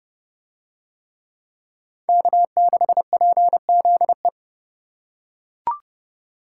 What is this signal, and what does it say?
Morse code, heard as a keyed tone, about 31 wpm, 700 Hz. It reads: K6PZE